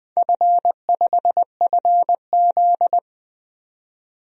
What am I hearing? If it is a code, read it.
Morse code, heard as a keyed tone, about 20 wpm, 700 Hz. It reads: F5FZ
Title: DXCC-100 Call Signs - 4 Characters - 20wpm